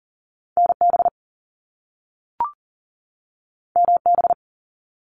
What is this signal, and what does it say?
Morse code, heard as a keyed tone, about 42 wpm, 700 Hz. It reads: NB MB
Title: Canadian Provinces and Territories 40wpm